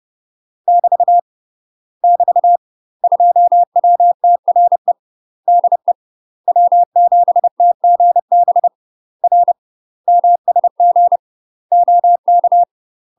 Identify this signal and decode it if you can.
Morse code, heard as a keyed tone, about 30 wpm, 700 Hz. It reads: <BT> <BT> 2WTRE DE W7TGB R MSG OK
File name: UK MASTER QSOs TRAINING / 30WPM / UK MASTER QSOs 0013